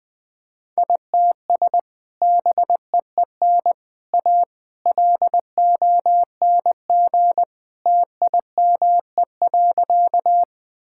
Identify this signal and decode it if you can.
Morse code, heard as a keyed tone, about 20 wpm, 700 Hz. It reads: ITS BEEN A LONG TIME.
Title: Sentences with 3-Word Phrases - 20wpm